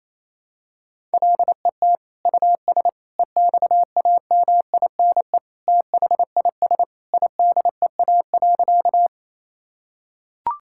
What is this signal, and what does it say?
Morse code, heard as a keyed tone, about 28 wpm, 700 Hz. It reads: LET US EXAMINE THIS IDEA.
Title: Sentences from Top 1600 Words 28wpm